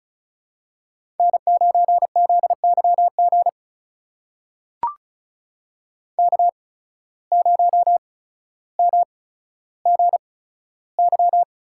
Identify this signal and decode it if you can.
Morse code, heard as a keyed tone, about 33 wpm, spaced as for 15 wpm, 700 Hz. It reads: N9ZYG K0MGY
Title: USA Call Signs - 5 Characters - 8x Character Spacing - 35wpm